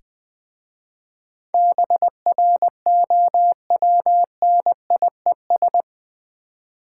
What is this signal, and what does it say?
Morse code, heard as a keyed tone, about 20 wpm, 700 Hz. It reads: BROWNIES
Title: Difficult Words with Hints - 20wpm